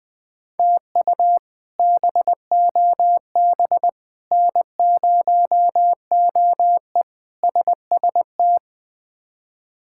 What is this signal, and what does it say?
Morse code, heard as a keyed tone, about 21 wpm, 700 Hz. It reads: TU BOB N0OE SST
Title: K1USN SST QSO Practice - 20wpm